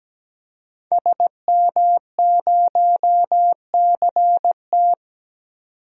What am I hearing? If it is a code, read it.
Morse code, heard as a keyed tone, about 17 wpm, 700 Hz. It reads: SM0CT
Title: DXCC-100 Call Signs - 5 Characters - 17wpm